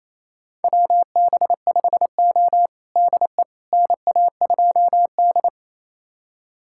Morse code, heard as a keyed tone, about 28 wpm, 690 Hz. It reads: WB5O DE NA2D
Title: USA Callsign DE Callsign - 4 Characters - 28wpm